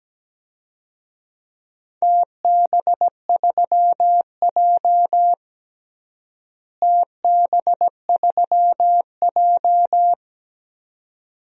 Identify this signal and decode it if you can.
Morse code, heard as a keyed tone, about 17 wpm, 700 Hz. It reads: TB3J TB3J
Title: DXCC-100 Call Signs - 4 Characters - Repeated 2x - 17wpm